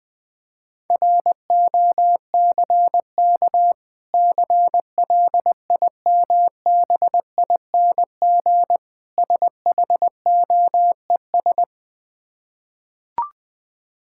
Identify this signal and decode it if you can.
Morse code, heard as a keyed tone, about 20 wpm, 700 Hz. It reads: ROCK CLIMBING SHOES
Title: Difficult Words with Hints - 20wpm